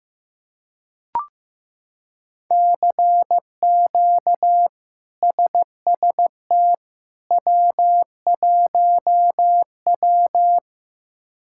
Morse code, heard as a keyed tone, about 15 wpm, 700 Hz. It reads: CQ SST W1W
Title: K1USN SST QSO Practice - 15wpm